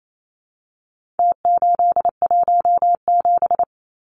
Morse code, heard as a keyed tone, about 28 wpm, 700 Hz. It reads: T817
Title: Numbers - Contest Style - 4-Digit 28wpm